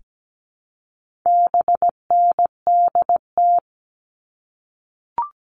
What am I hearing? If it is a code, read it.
Morse code, heard as a keyed tone, about 17 wpm, 700 Hz. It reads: BNDT
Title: All Cut Numbers - 4-Digit 17wpm